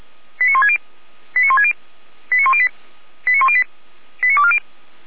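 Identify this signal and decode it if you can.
Five-tone selective call (ZVEI), heard as quick runs of five tones, tones of about 70 ms. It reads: 89160 89160 89108 89108 89240
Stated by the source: ZVEI five-tone selective call received with an SDR.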